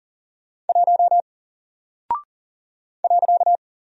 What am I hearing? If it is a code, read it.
Morse code, heard as a keyed tone, about 40 wpm, 700 Hz. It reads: 1 .
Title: Single Character (Letter:Number:Punctuation:Prosign) Alt 40wpm